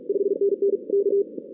Morse code, heard as a keyed tone, about 33 wpm, 410 Hz. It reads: 5NN K
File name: 2016 arrl dx cw / 2016-ARRL-DX-CW-K5TR- - 0-46